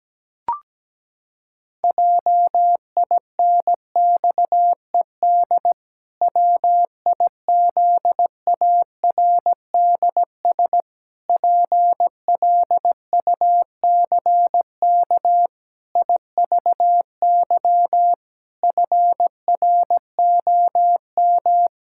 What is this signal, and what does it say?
Morse code, heard as a keyed tone, about 17 wpm, 700 Hz. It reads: JINXED WIZARDS PLUCK IVY FROM
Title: Pangram Challenge v2 - 17wpm